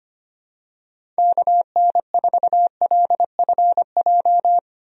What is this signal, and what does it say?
Morse code, heard as a keyed tone, about 25 wpm, 700 Hz. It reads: KN4LFJ
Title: USA Call Signs - 6 Characters - 25wpm